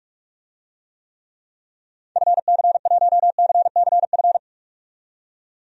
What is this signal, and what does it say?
Morse code, heard as a keyed tone, about 46 wpm, 700 Hz. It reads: FC1CCF
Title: DXCC-100 Call Signs - 6 Characters - 45wpm